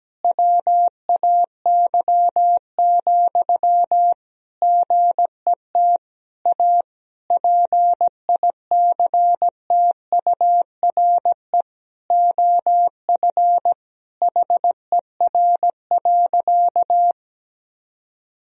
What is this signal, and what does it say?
Morse code, heard as a keyed tone, about 17 wpm, 700 Hz. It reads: WAY, GET A PICTURE OF HER.
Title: Sentences from Top 1000 Words - Review Part 1 - 17wpm